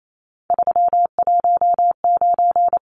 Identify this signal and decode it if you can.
Morse code, heard as a keyed tone, about 28 wpm, 700 Hz. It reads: 319
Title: Numbers - 3-Digit 28wpm